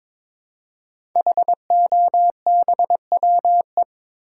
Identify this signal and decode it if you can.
Morse code, heard as a keyed tone, about 22 wpm, 700 Hz. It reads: HOBWE
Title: Code Groups - 5 Letters - 22wpm